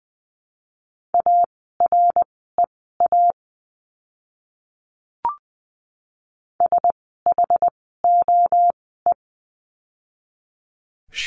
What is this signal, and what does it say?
Morse code, heard as a keyed tone, about 20 wpm, spaced as for 15 wpm, 700 Hz. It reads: AREA SHOE
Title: 068 Lesson 12f - Letter d - Words 20wpm